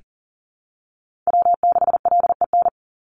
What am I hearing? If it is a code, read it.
Morse code, heard as a keyed tone, about 41 wpm, 700 Hz. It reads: W6LEN
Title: Call Signs - 40wpm